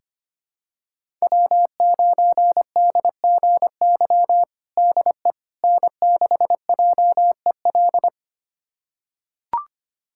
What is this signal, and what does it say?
Morse code, heard as a keyed tone, about 25 wpm, 700 Hz. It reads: W9DGY DE N6JEL
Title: USA Callsign DE Callsign - 5 Characters - 25wpm